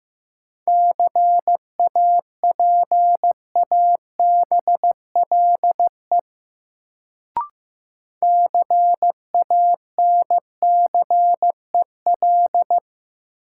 Morse code, heard as a keyed tone, about 15 wpm, 700 Hz. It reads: CAPABLE CANCEL
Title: Sentences from Top 1200 Words - New Word Drill Alt - 15wpm